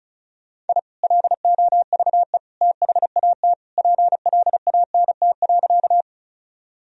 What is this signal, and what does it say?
Morse code, heard as a keyed tone, about 36 wpm, 700 Hz. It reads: I LOVE THAT PLANT.
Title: Sentences from Top 1000 Words - Review Part 1 - 35wpm